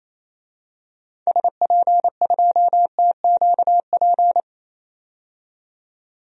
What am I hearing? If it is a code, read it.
Morse code, heard as a keyed tone, about 28 wpm, 700 Hz. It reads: SP2TQP
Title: DXCC-100 Call Signs - 6 Characters - 28wpm